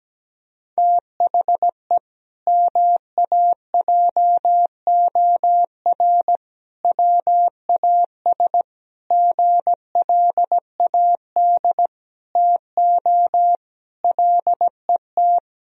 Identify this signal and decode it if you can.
Morse code, heard as a keyed tone, about 17 wpm, 700 Hz. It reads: THE MAJOR WAS GLAD TO LET
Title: Sentences from Top 1000 Words - Review Part 4 - 17wpm